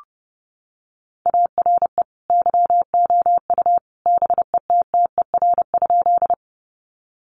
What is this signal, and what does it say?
Morse code, heard as a keyed tone, about 30 wpm, 700 Hz. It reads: ARE YOU BETTER?
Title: Sentences from Top 400 Words 30wpm